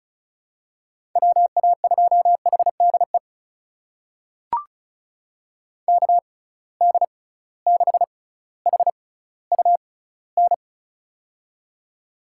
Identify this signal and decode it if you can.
Morse code, heard as a keyed tone, about 36 wpm, spaced as for 18 wpm, 700 Hz. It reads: WA2HDE KD6HUN
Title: USA Call Signs - 6 Characters - 6x Character Spacing - 35wpm